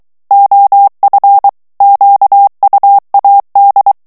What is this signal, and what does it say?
Morse code, heard as a keyed tone, about 23 wpm, 800 Hz. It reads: OF QUAD
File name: HOLLANDAIS texte clair 1 vitesse 22 mots minute